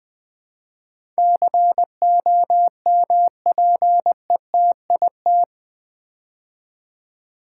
Morse code, heard as a keyed tone, about 20 wpm, 700 Hz. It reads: COMPETIT
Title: Build A Word - 4 Letters At A Time - 11-Letter Words - 20wpm